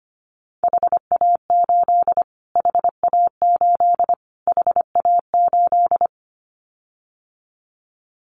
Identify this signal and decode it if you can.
Morse code, heard as a keyed tone, about 25 wpm, 700 Hz. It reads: HA8 HA8 HA8
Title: Callsign Prefixes for Top 25 DXCC - Repeated 3x - 25wpm